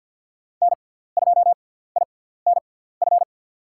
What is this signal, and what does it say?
Morse code, heard as a keyed tone, about 49 wpm, spaced as for 17 wpm, 690 Hz. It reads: N2INF
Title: USA Call Signs - 6x Character Spacing - 50wpm